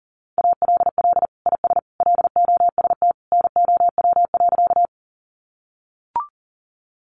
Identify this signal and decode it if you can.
Morse code, heard as a keyed tone, about 37 wpm, 700 Hz. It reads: ALL IS LOST NOW.
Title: Sentences from Top 1400 Words 40wpm